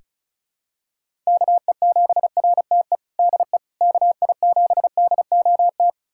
Morse code, heard as a keyed tone, about 34 wpm, 700 Hz. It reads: KE7RTE DE KI7DOT
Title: USA Callsign DE Callsign - 6 Characters - 35wpm